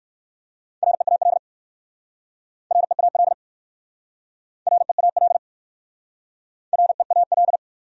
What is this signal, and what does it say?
Morse code, heard as a keyed tone, about 59 wpm, 700 Hz. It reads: REAL REAL REAL REAL